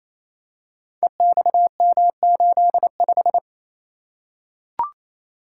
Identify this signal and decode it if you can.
Morse code, heard as a keyed tone, about 27 wpm, 700 Hz. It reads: EXM85
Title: Code Groups - 5 Letter:Numbers - 28wpm